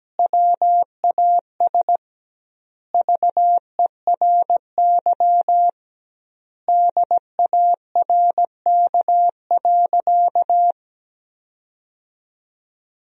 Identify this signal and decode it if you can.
Morse code, heard as a keyed tone, about 17 wpm, 700 Hz. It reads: WAS VERY DARK.
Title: 2x Word-Spacing - Sentences from Top 500 Words - 17wpm